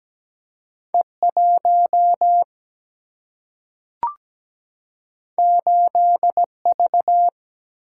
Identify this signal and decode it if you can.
Morse code, heard as a keyed tone, about 17 wpm, 700 Hz. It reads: E1 8V
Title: Single-Letter Single-Number 17wpm